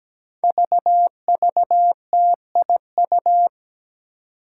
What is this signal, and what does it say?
Morse code, heard as a keyed tone, about 17 wpm, 700 Hz. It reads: VVTIU
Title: Five Challenging Letter-Number Combinations 17wpm